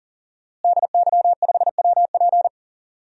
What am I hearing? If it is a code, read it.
Morse code, heard as a keyed tone, about 40 wpm, 700 Hz. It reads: DY5WP
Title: DXCC-100 Call Signs - 5 Characters - Repeated 3x - 40wpm